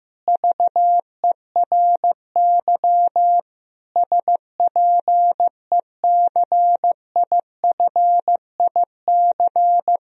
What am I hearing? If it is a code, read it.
Morse code, heard as a keyed tone, about 15 wpm, 700 Hz. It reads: VERY SPECIFIC
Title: Sentences from Top 1000 Words - Review Part 3 - 15wpm